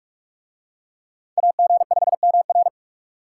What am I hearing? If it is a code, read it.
Morse code, heard as a keyed tone, about 41 wpm, 700 Hz. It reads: AG5MR